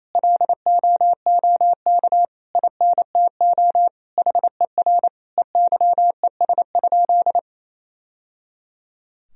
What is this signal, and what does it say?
Morse code, heard as a keyed tone, about 28 wpm, 700 Hz. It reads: LOOK INTO HER EYES?
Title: Sentences from Top 1000 Words - Review Part 4 - 28wpm